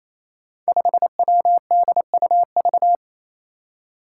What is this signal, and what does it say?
Morse code, heard as a keyed tone, about 27 wpm, 700 Hz. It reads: 5WDUV
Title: Five Challenging Letter-Number Combinations 28wpm